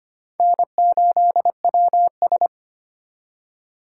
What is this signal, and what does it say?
Morse code, heard as a keyed tone, about 34 wpm, spaced as for 30 wpm, 700 Hz. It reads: N8WS
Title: Speed Racing - USA 1x2 Call Signs 25wpm